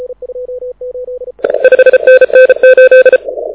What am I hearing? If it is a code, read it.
Morse code, heard as a keyed tone, about 33 wpm, 1550 Hz, with noise. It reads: 5NN 8
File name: audio zawody / IARU 2015 CW SN7Q